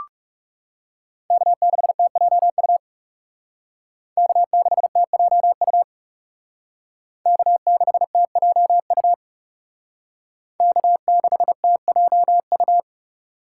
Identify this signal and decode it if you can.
Morse code, heard as a keyed tone, about 38 wpm, 700 Hz. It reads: K6TJU K6TJU K6TJU K6TJU